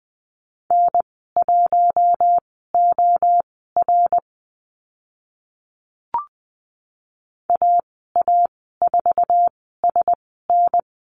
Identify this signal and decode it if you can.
Morse code, heard as a keyed tone, about 20 wpm, spaced as for 14 wpm, 700 Hz. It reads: N1OR AA4SN
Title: 062 Lesson 11g - Letter h - Call Signs 20wpm